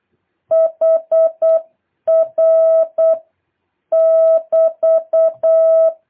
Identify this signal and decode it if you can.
Morse code, heard as a keyed tone, about 8 wpm, 650 Hz. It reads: HR <BT>
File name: jh8cbh / 2023cw